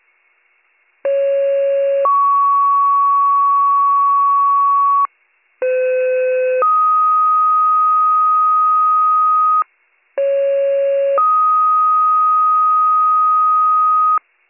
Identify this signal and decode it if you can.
Two-tone page, heard as a one-second tone, then a three-second tone, about 550 then 1120 Hz, about 510 then 1250 Hz, about 550 then 1250 Hz. It is heard from 1.0 to 14.2 s.